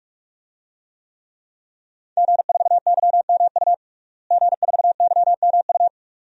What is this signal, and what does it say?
Morse code, heard as a keyed tone, about 46 wpm, 700 Hz. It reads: G4YMU G4YMU
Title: DXCC-100 Call Signs - 5 Characters - Repeated 2x - 45wpm